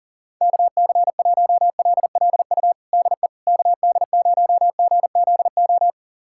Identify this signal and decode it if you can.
Morse code, heard as a keyed tone, about 37 wpm, 700 Hz. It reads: KC1LLU DE KD0GZO
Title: USA Callsign DE Callsign 40wpm